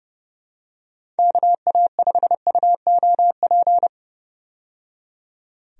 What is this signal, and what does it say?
Morse code, heard as a keyed tone, about 31 wpm, 700 Hz. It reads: KA5UOP
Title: USA Call Signs - 6 Characters - 30wpm